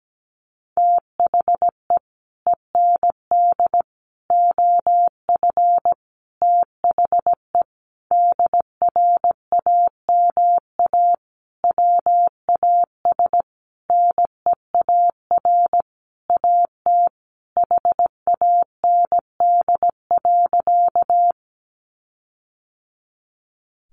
Morse code, heard as a keyed tone, about 17 wpm, 700 Hz. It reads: THE END OF THE DRAMA WAS NEAR AT HAND.
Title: Sentences from Top 1500 Words 17wpm